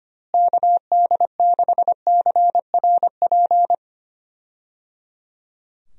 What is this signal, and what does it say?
Morse code, heard as a keyed tone, about 25 wpm, 700 Hz. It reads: KD6CRP
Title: USA Call Signs - 6 Characters - Repeated 3x - 25wpm